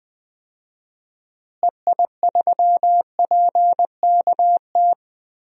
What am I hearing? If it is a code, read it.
Morse code, heard as a keyed tone, about 20 wpm, 700 Hz. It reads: EI3PKT
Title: DXCC-100 Call Signs - 6 Characters - Repeated 2x - 20wpm